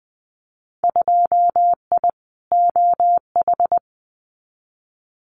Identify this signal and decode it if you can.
Morse code, heard as a keyed tone, about 20 wpm, 700 Hz. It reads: 2I OH
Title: Winter Field Day Exchanges 20wpm